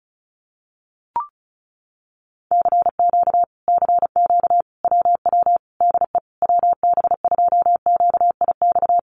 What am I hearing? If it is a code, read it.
Morse code, heard as a keyed tone, about 34 wpm, 700 Hz. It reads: CQ CQ WW DE WB2QIX
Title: Common QSO Phrases Encore 35wpm